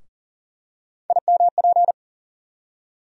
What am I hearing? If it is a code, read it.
Morse code, heard as a keyed tone, about 40 wpm, 700 Hz. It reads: IMP
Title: USA Call Sign Postfixes - 3 Letters 40wpm